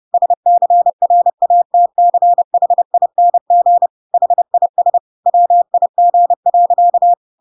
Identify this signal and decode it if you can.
Morse code, heard as a keyed tone, about 30 wpm, 700 Hz. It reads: SCRATCHING HIS WIG.
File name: Pinocchio / Pinocchio ch1 30 30 ik1hge